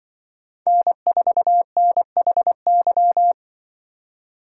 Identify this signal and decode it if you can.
Morse code, heard as a keyed tone, about 24 wpm, 700 Hz. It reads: N4NHY
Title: Speed Racing - USA 1x3 Call Signs 20wpm